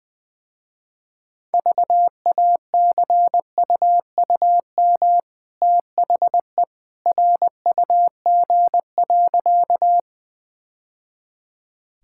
Morse code, heard as a keyed tone, about 20 wpm, 700 Hz. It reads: VACUUM THE RUG.
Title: Easy Sentences with Hints - 20wpm